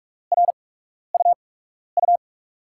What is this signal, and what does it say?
Morse code, heard as a keyed tone, about 44 wpm, 700 Hz. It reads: R U U
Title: USA Call Signs - 8x Character Spacing - 45wpm